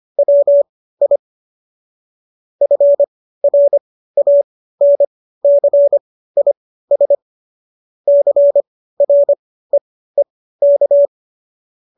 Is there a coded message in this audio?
Yes: Morse code from 0.2 to 11.1 s, heard as a keyed tone, about 25 wpm, spaced as for 15 wpm, 570 Hz. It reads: WI FRANCIS CREEK